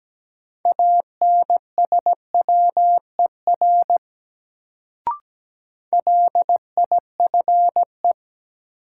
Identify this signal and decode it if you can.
Morse code, heard as a keyed tone, about 17 wpm, 700 Hz. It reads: ANSWER LIFE
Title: Sentences from Top 300 Words - New Word Drill - 17wpm